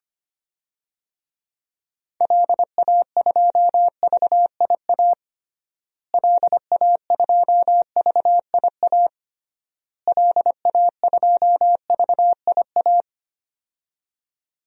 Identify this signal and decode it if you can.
Morse code, heard as a keyed tone, about 25 wpm, 700 Hz. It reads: LA2VIA LA2VIA LA2VIA
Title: DXCC-100 Call Signs - 6 Characters - Repeated 3x - 25wpm